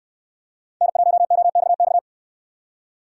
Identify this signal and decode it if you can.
Morse code, heard as a keyed tone, about 68 wpm, 700 Hz. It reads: N1OXX